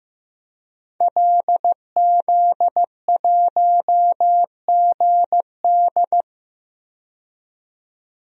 Morse code, heard as a keyed tone, about 15 wpm, 700 Hz. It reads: LZ1GD
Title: DXCC-100 Call Signs - 5 Characters - 15wpm